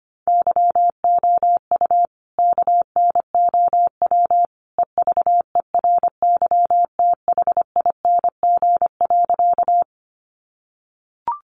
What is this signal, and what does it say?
Morse code, heard as a keyed tone, about 25 wpm, 700 Hz. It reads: YOU KNOW EVERYTHING.
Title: Sentences from Top 700 Words 25wpm